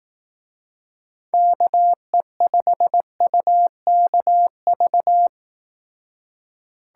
Morse code, heard as a keyed tone, about 18 wpm, 700 Hz. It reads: KE5UKV
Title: Speed Racing - USA 2x3 Call Signs 15wpm